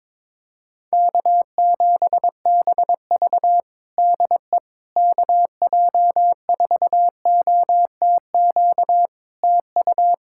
Morse code, heard as a keyed tone, about 22 wpm, 700 Hz. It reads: K7BV DE KJ4OTQ TU
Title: Practice QSO - Exchange 4 - Ending - 22wpm